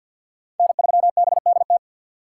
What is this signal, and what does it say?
Morse code, heard as a keyed tone, about 53 wpm, 700 Hz. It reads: N3BDT
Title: USA Call Signs - 3x Character Spacing - 50wpm